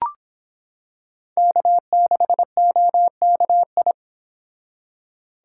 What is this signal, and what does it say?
Morse code, heard as a keyed tone, about 26 wpm, 700 Hz. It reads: K6OKI